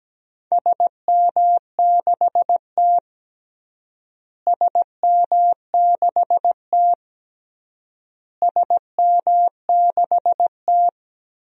Morse code, heard as a keyed tone, about 17 wpm, 700 Hz. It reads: SM6T SM6T SM6T
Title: DXCC-100 Call Signs - 4 Characters - Repeated 3x - 17wpm